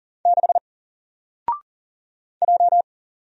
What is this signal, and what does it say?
Morse code, heard as a keyed tone, about 40 wpm, 700 Hz. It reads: 6 J
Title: Single Letter-Number Alt 40wpm